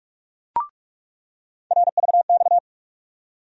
Morse code, heard as a keyed tone, about 46 wpm, 700 Hz. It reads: RVX
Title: USA Call Sign Postfixes - 3 Letters 45wpm